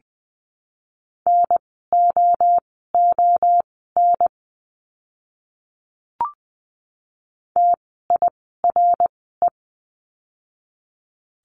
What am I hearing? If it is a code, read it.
Morse code, heard as a keyed tone, about 20 wpm, spaced as for 14 wpm, 700 Hz. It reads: NOON TIRE
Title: 061 Lesson 11f - Letter h - Words 20wpm